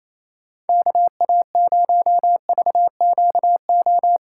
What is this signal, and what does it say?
Morse code, heard as a keyed tone, about 28 wpm, 700 Hz. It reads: KA0VQO
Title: USA Call Signs - 6 Characters - 3x Character Spacing - 28wpm